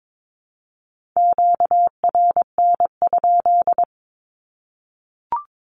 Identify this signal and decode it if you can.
Morse code, heard as a keyed tone, about 22 wpm, 700 Hz. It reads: QRN?
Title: Q-Codes Repeated 3x 22wpm